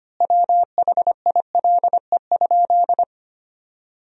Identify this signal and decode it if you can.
Morse code, heard as a keyed tone, about 25 wpm, 700 Hz. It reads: WHILE?
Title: 2x Word-Spacing - Sentences from Top 300 Words - 25wpm